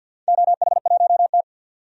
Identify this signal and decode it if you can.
Morse code, heard as a keyed tone, about 50 wpm, 700 Hz. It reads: OH1T
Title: DXCC-100 Call Signs - 4 Characters - 50wpm